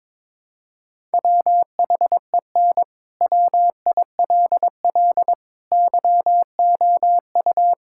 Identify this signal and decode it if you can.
Morse code, heard as a keyed tone, about 22 wpm, 700 Hz. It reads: WHEN WILL YOU
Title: Sentences from Top 1000 Words - Review Part 6 - 22wpm